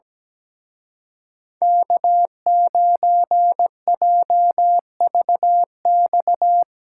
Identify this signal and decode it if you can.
Morse code, heard as a keyed tone, about 18 wpm, 700 Hz. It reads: K9JVX
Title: Speed Racing - USA 1x3 Call Signs 17wpm